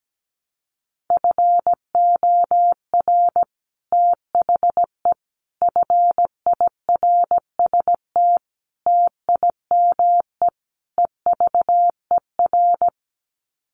Morse code, heard as a keyed tone, about 17 wpm, 700 Hz. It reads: FOR THE FIRST TIME EVER
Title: Sets of 5 Words Encore2 17wpm